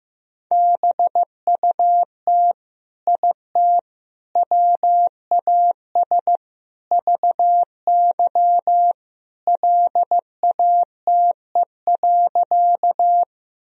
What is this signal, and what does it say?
Morse code, heard as a keyed tone, about 16 wpm, 700 Hz. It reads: BUT IT WAS VY LATE.
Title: Sentences from Top 300 Words and Abbreviations 15wpm